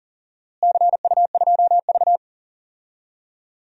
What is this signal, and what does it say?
Morse code, heard as a keyed tone, about 40 wpm, 700 Hz. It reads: CU2V
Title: DXCC-100 Call Signs - 4 Characters - Repeated 2x - 40wpm